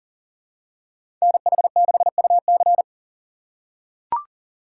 Morse code, heard as a keyed tone, about 40 wpm, 700 Hz. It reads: NH6UC
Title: USA Call Signs - 5 Characters - 3x Character Spacing - 40wpm